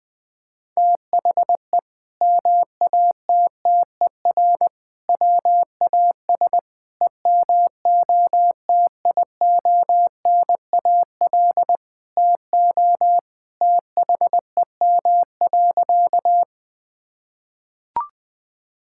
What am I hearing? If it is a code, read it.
Morse code, heard as a keyed tone, about 20 wpm, 700 Hz. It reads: THE MATTER WAS EMOTIONAL TO THEM.
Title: Sentences from Top 1000 Words - Review Part 5 - 20wpm